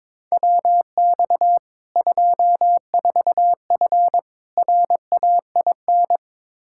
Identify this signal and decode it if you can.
Morse code, heard as a keyed tone, about 22 wpm, 700 Hz. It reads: WX 24F RAIN
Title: Practice QSO - Exchange 2 - Info - 22wpm